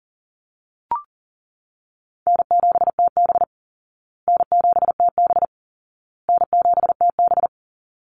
Morse code, heard as a keyed tone, about 40 wpm, 700 Hz. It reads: N7TB N7TB N7TB
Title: Easy Call Signs Repeated 3x - 40wpm